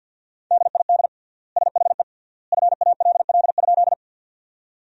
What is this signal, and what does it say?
Morse code, heard as a keyed tone, about 47 wpm, 700 Hz. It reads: DID SHE FALL?
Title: 3x Word-Spacing - Sentences from Top 500 Words - 50wpm